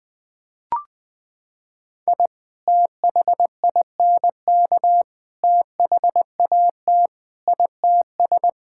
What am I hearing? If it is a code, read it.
Morse code, heard as a keyed tone, about 20 wpm, 700 Hz. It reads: I THINK THAT ITS
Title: Sentences with 2-Word Phrases - 20wpm